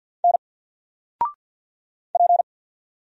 Morse code, heard as a keyed tone, about 50 wpm, 700 Hz. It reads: N P
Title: Single Letter-Number 50wpm